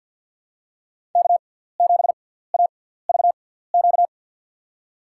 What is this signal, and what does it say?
Morse code, heard as a keyed tone, about 52 wpm, spaced as for 17 wpm, 700 Hz. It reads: K7AVQ
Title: USA Call Signs - 5 Characters - 6x Character Spacing - 50wpm